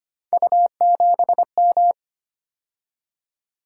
Morse code, heard as a keyed tone, about 25 wpm, 700 Hz. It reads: U7M
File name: Three Letter-Number Combinations - 25wpm